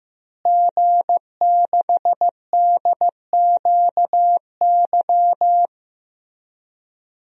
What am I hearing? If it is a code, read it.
Morse code, heard as a keyed tone, about 15 wpm, 700 Hz. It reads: G6DQY
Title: DXCC-100 Call Signs - 5 Characters - Repeated 2x - 15wpm